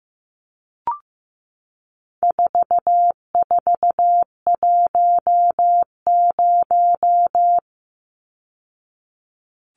Morse code, heard as a keyed tone, about 15 wpm, 700 Hz. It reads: 4410
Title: Numbers - 4-Digit 15wpm